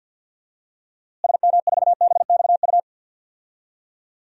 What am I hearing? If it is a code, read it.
Morse code, heard as a keyed tone, about 48 wpm, 700 Hz. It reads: SM4BXU
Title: DXCC-100 Call Signs - 6 Characters - Repeated 3x - 50wpm